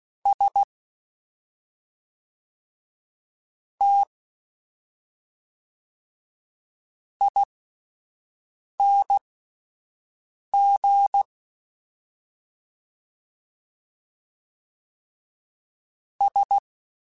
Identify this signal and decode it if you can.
Morse code, heard as a keyed tone, about 16 wpm, spaced as for 5 wpm, 800 Hz. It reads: S T ING S
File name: Misc / Misc-mp3s / Jumps @5